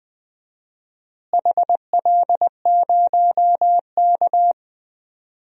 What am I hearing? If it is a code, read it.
Morse code, heard as a keyed tone, about 20 wpm, 700 Hz. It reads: HL0K
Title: DXCC-100 Call Signs - 4 Characters - Repeated 3x - 20wpm